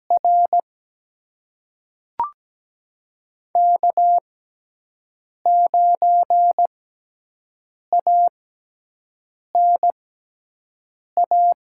Morse code, heard as a keyed tone, about 17 wpm, spaced as for 6 wpm, 700 Hz. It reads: R K9ANA
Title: USA Call Signs - 5 Characters - 6x Character Spacing - 17wpm